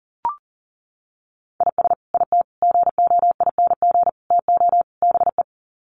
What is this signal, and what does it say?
Morse code, heard as a keyed tone, about 37 wpm, 700 Hz. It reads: IS IT GOING TO BE
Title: Sets of 5 Words Encore2 40wpm